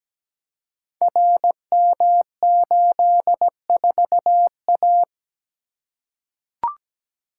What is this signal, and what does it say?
Morse code, heard as a keyed tone, about 17 wpm, 700 Hz. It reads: RM84A
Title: Code Groups - 5 Letter:Numbers - 17wpm